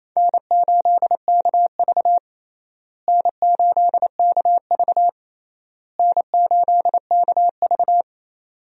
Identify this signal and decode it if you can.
Morse code, heard as a keyed tone, about 28 wpm, 700 Hz. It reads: N8KV N8KV N8KV
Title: USA Call Signs - 4 Characters - Repeated 3x - 28wpm